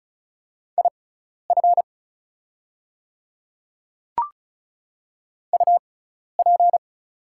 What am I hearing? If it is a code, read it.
Morse code, heard as a keyed tone, about 36 wpm, spaced as for 12 wpm, 690 Hz. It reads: IF UP
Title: 6x Character Spacing - Top 100 Words - 35wpm